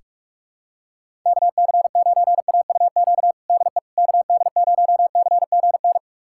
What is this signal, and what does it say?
Morse code, heard as a keyed tone, about 41 wpm, 700 Hz. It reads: KC9AUQ DE KD0CGN
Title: USA Callsign DE Callsign - 6 Characters - 45wpm